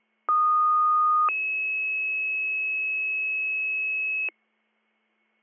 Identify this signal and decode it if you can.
Two-tone page, heard as a one-second tone, then a three-second tone, about 1230 then 2600 Hz, with noise. It is heard from 0.3 to 4.3 s.